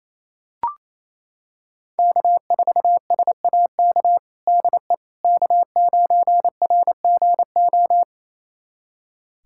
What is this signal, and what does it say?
Morse code, heard as a keyed tone, about 28 wpm, 690 Hz. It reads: K4SAK DE K9RGO
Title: USA Callsign DE Callsign - 5 Characters - 28wpm